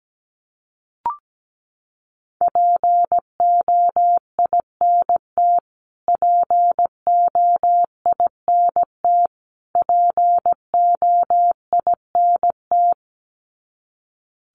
Morse code, heard as a keyed tone, about 17 wpm, 700 Hz. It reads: POINT POINT POINT
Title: Top 500 Words Repeated 17wpm